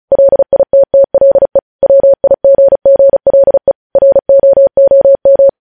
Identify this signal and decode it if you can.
Morse code, heard as a keyed tone, about 36 wpm, 550 Hz. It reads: LITTLE WIGGLE ROOM
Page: news headlines in morse code at 35 wpm